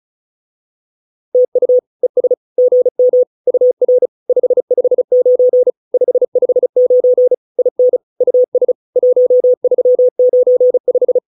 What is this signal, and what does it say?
Morse code, heard as a keyed tone, about 34 wpm, 500 Hz. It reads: TU ES GM UR 559 559 IN US 1395